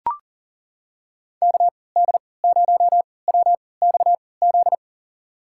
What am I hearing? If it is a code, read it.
Morse code, heard as a keyed tone, about 40 wpm, spaced as for 23 wpm, 700 Hz. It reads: KD0WXZ
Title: USA Call Signs - 6 Characters - 3x Character Spacing - 40wpm